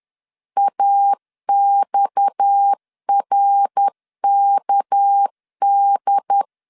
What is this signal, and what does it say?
Morse code, heard as a keyed tone, about 10 wpm, 800 Hz. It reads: AXRKD
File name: CW / DET / 1 6 1500C S B 10wpm 4137